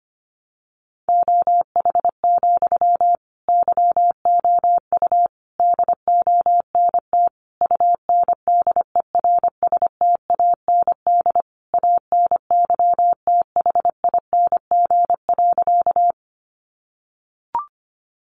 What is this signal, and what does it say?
Morse code, heard as a keyed tone, about 25 wpm, 700 Hz. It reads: OH, YOU DONT UNDERSTAND ANYTHING.
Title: Sentences from Top 1000 Words 25wpm